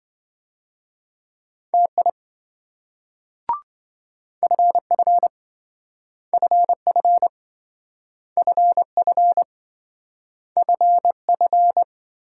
Morse code, heard as a keyed tone, about 26 wpm, 700 Hz. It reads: TI FF FF FF FF